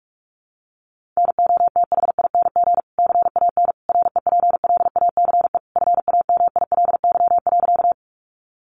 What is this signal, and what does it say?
Morse code, heard as a keyed tone, about 44 wpm, 700 Hz. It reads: NOTHING CAN REPLACE FAMILY.
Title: Sentences from Top 1400 Words 45wpm